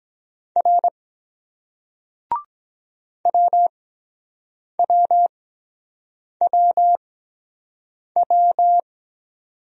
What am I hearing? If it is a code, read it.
Morse code, heard as a keyed tone, about 22 wpm, spaced as for 7 wpm, 700 Hz. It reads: R WWWW